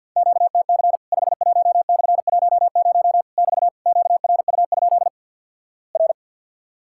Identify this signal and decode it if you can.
Morse code, heard as a keyed tone, about 50 wpm, 700 Hz. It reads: QTC 51/10 <BT> QRU? R
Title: Worked All Europe DX Contest - QTC Exchange with Repeats - 50wpm